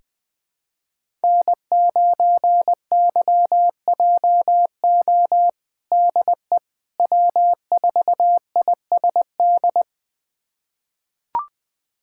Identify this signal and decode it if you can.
Morse code, heard as a keyed tone, about 20 wpm, 700 Hz. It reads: N9YJO DE W4ISD
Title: USA Callsign DE Callsign - 5 Characters - 20wpm